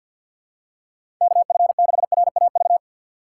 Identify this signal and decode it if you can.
Morse code, heard as a keyed tone, about 53 wpm, 700 Hz. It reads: KF6RAV
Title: USA Call Signs - 3x Character Spacing - 50wpm